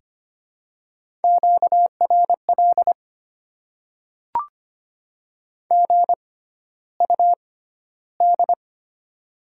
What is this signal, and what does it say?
Morse code, heard as a keyed tone, about 25 wpm, spaced as for 13 wpm, 700 Hz. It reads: QRL GUD
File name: Top 20 QSO Elements - 6x Character Spacing - 25wpm